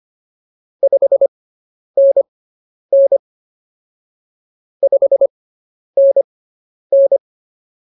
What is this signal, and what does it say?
Morse code, heard as a keyed tone, about 24 wpm, spaced as for 10 wpm, 550 Hz. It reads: 5NN 5NN